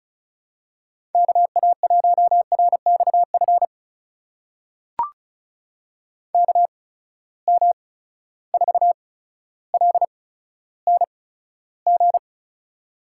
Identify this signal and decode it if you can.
Morse code, heard as a keyed tone, about 36 wpm, spaced as for 15 wpm, 700 Hz. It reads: KA1RXF KM4LNG
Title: USA Call Signs - 6 Characters - 8x Character Spacing - 35wpm